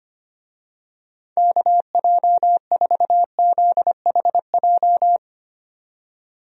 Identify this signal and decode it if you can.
Morse code, heard as a keyed tone, about 25 wpm, 700 Hz. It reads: KJ4ZHJ
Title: USA Call Signs - 6 Characters - 25wpm